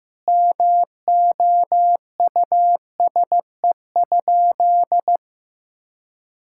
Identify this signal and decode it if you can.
Morse code, heard as a keyed tone, about 15 wpm, 700 Hz. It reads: MOUSE?
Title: Sentences from Top 1000 Words - Review Part 5 - 15wpm